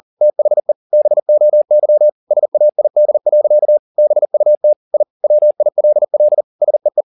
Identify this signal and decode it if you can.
Morse code, heard as a keyed tone, about 40 wpm, 600 Hz. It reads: THE BOY SAID. BUT I WILL SEE